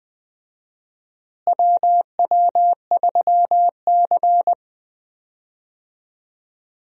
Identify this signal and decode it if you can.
Morse code, heard as a keyed tone, about 20 wpm, 700 Hz. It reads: WW3C
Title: USA Call Signs - 4 Characters - Repeated 3x - 20wpm